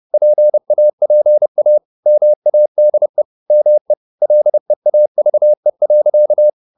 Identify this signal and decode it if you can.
Morse code, heard as a keyed tone, about 30 wpm, 600 Hz. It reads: PAPA MADE ME LEAVE.
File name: the old man and the sea / the old man and the sea 1 30wpm